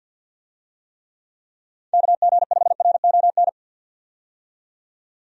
Morse code, heard as a keyed tone, about 47 wpm, 700 Hz. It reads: KG5RON
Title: USA Call Signs - 6 Characters - 50wpm